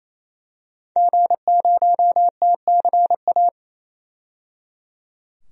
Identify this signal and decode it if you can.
Morse code, heard as a keyed tone, about 28 wpm, 700 Hz. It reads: G0TCA
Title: DXCC-100 Call Signs - 5 Characters - 28wpm